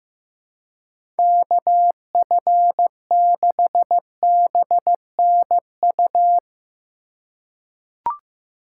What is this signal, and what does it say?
Morse code, heard as a keyed tone, about 15 wpm, 700 Hz. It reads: KF6BNU
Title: USA Call Signs - 2x Character Spacing - 15wpm